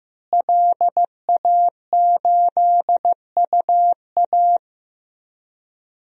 Morse code, heard as a keyed tone, about 15 wpm, 700 Hz. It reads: LA8UA
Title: DXCC-100 Call Signs - 5 Characters - Repeated 3x - 15wpm